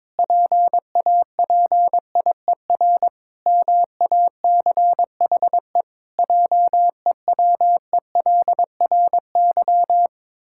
Morse code, heard as a keyed tone, about 22 wpm, 700 Hz. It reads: PAPIER MACHE JEWELRY
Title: Pangram Challenge v2 - 22wpm